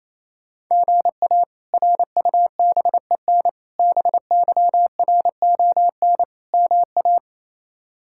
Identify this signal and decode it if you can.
Morse code, heard as a keyed tone, about 28 wpm, 700 Hz. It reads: GA RUBEN BYRON MA
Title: K1USN SST QSO Practice - 28wpm